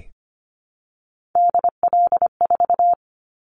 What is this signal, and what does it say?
Morse code, heard as a keyed tone, about 25 wpm, 700 Hz. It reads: DL4
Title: Callsign Prefixes for Top 50 DXCC - 25wpm